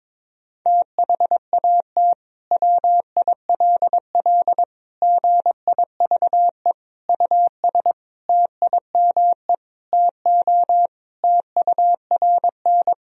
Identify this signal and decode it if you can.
Morse code, heard as a keyed tone, about 22 wpm, 700 Hz. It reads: THAT WILL GIVE US TIME TO TURN
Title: Sentences from Top 1000 Words - Review Part 1 - 22wpm